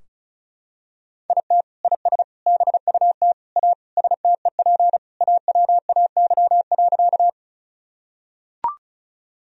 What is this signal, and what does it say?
Morse code, heard as a keyed tone, about 36 wpm, 700 Hz. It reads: IT IS BUT A STEP AWAY.
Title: Sentences from Top 1000 Words - Review Part 2 - 35wpm